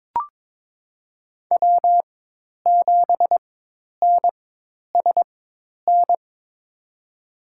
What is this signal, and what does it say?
Morse code, heard as a keyed tone, about 22 wpm, spaced as for 10 wpm, 700 Hz. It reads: W7NSN
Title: USA Call Signs - 4x Character Spacing - 22wpm